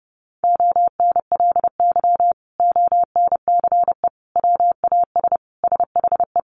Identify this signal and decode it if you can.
Morse code, heard as a keyed tone, about 30 wpm, 700 Hz. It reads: ONLY ONCE WAS SHE
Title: Sentences from Top 1100 Words 30wpm